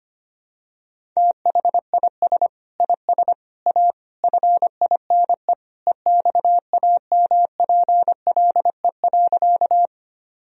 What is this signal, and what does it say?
Morse code, heard as a keyed tone, about 25 wpm, 700 Hz. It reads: THIS IS A FINE EXAMPLE.
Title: Sentences from Top 1000 Words - Review Part 1 - 25wpm